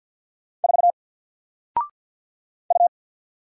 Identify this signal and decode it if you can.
Morse code, heard as a keyed tone, about 48 wpm, 700 Hz. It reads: 4 U
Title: Single Character (Letter:Number:Punctuation:Prosign) 50wpm